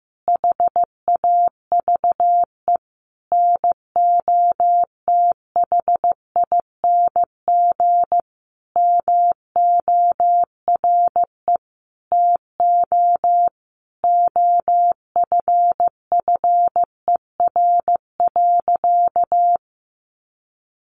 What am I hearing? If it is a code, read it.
Morse code, heard as a keyed tone, about 15 wpm, 700 Hz. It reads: HAVE NOTHING MORE TO OFFER.
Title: Sentences from Top 1100 Words 15wpm